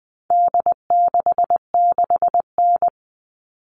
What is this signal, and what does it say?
Morse code, heard as a keyed tone, about 20 wpm, 700 Hz. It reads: D66N
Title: All Cut Numbers - 4-Digit 20wpm